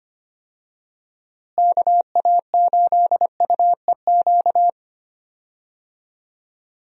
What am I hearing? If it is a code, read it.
Morse code, heard as a keyed tone, about 25 wpm, 700 Hz. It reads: KA8UEQ
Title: USA Call Signs - 6 Characters - Repeated 3x - 25wpm